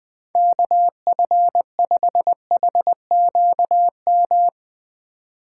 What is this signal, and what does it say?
Morse code, heard as a keyed tone, about 20 wpm, 700 Hz. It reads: KF5HQM